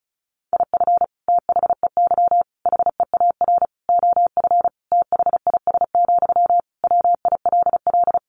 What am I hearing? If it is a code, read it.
Morse code, heard as a keyed tone, about 34 wpm, 700 Hz. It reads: IF THEY HEAR OF THIS, WILL
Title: Sentences from Top 400 Words 35wpm